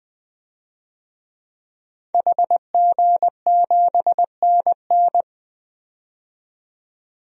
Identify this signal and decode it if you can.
Morse code, heard as a keyed tone, about 20 wpm, 700 Hz. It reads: HG7NN
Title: DXCC-100 Call Signs - 5 Characters - 20wpm